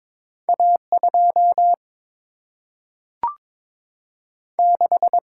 Morse code, heard as a keyed tone, about 22 wpm, 700 Hz. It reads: A2 6
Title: Single-Letter Single-Number 22wpm